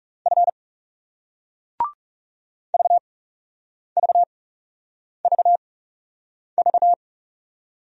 Morse code, heard as a keyed tone, about 40 wpm, spaced as for 8 wpm, 700 Hz. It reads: F VVVV